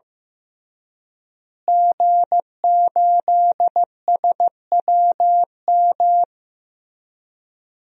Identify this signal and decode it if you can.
Morse code, heard as a keyed tone, about 16 wpm, 700 Hz. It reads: G8SWM
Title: DXCC-100 Call Signs - 5 Characters - Repeated 3x - 15wpm